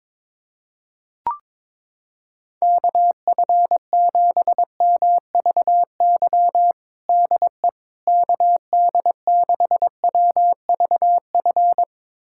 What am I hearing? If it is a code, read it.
Morse code, heard as a keyed tone, about 22 wpm, 700 Hz. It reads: KF7MVY DE KD6WVF
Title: USA Callsign DE Callsign - 6 Characters - 22wpm